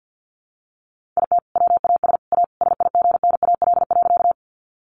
Morse code, heard as a keyed tone, about 50 wpm, 700 Hz. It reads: IT WAS A SIGNAL.
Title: Sentences from Top 1400 Words 50wpm